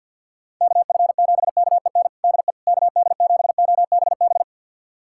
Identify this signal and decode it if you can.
Morse code, heard as a keyed tone, about 51 wpm, 690 Hz. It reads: KF7KEN DE KD7OBB
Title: USA Callsign DE Callsign - 6 Characters - 50wpm